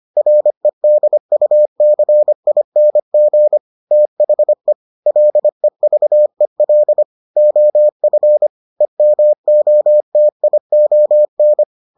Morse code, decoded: REDUCING THE LEVEL OF EMOTION

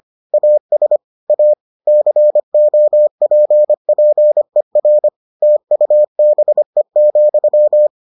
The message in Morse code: AS A COPPER TUBE,

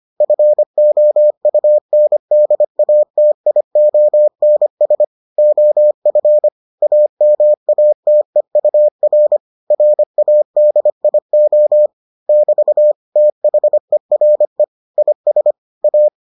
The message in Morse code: FOUNDATIONS OF AMATEUR RADIO <BT> THERE IS A

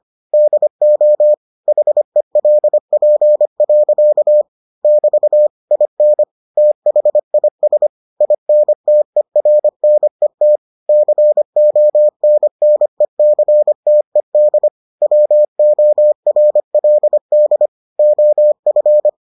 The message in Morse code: DO HELP. <BT> IN THIS INTERNET CONNECTED WORLD OF

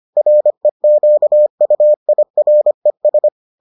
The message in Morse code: REQUIRES

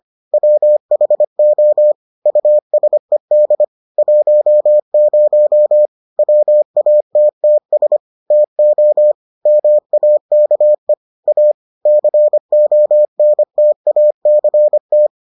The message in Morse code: WHO USED 10 WATTS TO MAKE A CONTACT

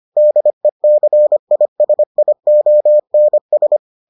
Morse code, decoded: DECISIONS

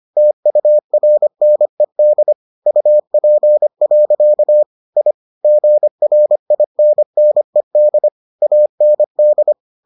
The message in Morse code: TURNED UP. I GRINNED AND

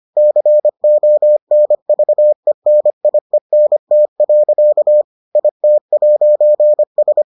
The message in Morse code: CONVENIENT. IT'S

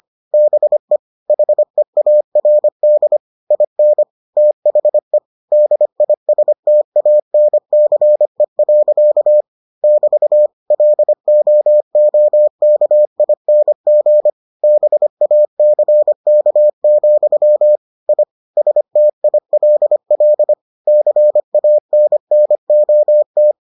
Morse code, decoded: BE HEARD IN THE DISTANCE. <BT> LOOKING BACK, I STILL CANNOT

